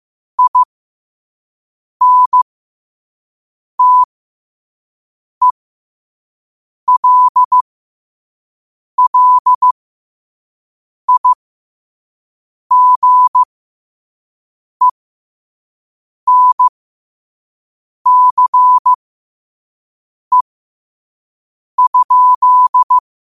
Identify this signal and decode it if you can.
Morse code, heard as a keyed tone, about 15 wpm, spaced as for 5 wpm, 1000 Hz. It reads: INTELLIGENCE?
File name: qotdfiles / 5wpm